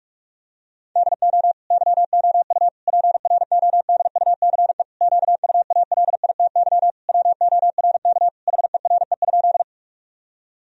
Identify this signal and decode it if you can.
Morse code, heard as a keyed tone, about 46 wpm, 700 Hz. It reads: DO YOU PRODUCE QUALITY WORK HERE?
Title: Sentences from Top 1000 Words - Review Part 4 - 45wpm